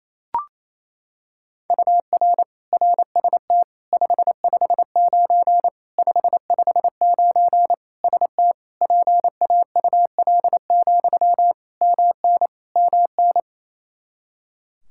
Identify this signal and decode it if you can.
Morse code, heard as a keyed tone, about 28 wpm, 700 Hz. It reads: UR RST 559 559 ST PAUL, MN MN
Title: Practice QSO - Exchange 1 - Beginning - 28wpm